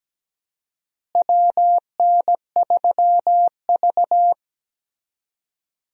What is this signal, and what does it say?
Morse code, heard as a keyed tone, about 17 wpm, 700 Hz. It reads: WN3V
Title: USA Call Signs - 4 Characters - Repeated 3x - 17wpm